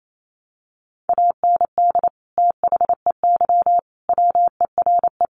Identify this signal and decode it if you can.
Morse code, heard as a keyed tone, about 28 wpm, 700 Hz. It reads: AND THEY WERE
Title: Sentences from Top 1300 Words 28wpm